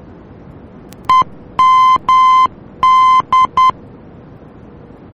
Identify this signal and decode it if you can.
Morse code, heard as a keyed tone, about 10 wpm, 1030 Hz. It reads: EMD
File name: germany / mx / EMD-D